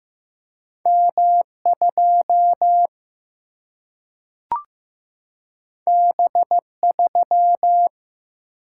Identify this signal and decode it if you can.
Morse code, heard as a keyed tone, about 15 wpm, 700 Hz. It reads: M2 B3
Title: Single-Letter Single-Number 15wpm